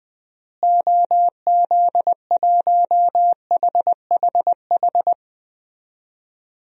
Morse code, heard as a keyed tone, about 20 wpm, 700 Hz. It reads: OZ1HHH
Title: DXCC-100 Call Signs - 6 Characters - Repeated 2x - 20wpm